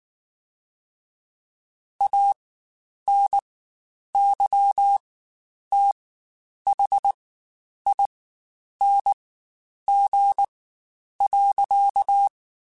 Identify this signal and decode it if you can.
Morse code, heard as a keyed tone, about 19 wpm, spaced as for 9 wpm, 800 Hz. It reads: ANYTHING.